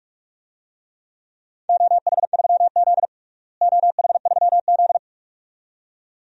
Morse code, heard as a keyed tone, about 43 wpm, 700 Hz. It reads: OH3Z OH3Z
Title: DXCC-100 Call Signs - 4 Characters - Repeated 2x - 45wpm